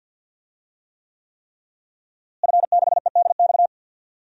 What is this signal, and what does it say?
Morse code, heard as a keyed tone, about 46 wpm, 700 Hz. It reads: F6EDX
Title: DXCC-100 Call Signs - 5 Characters - Repeated 3x - 50wpm